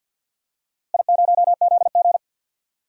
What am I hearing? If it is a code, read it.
Morse code, heard as a keyed tone, about 52 wpm, 700 Hz. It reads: I0ZG
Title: DXCC-100 Call Signs - 4 Characters - Repeated 3x - 50wpm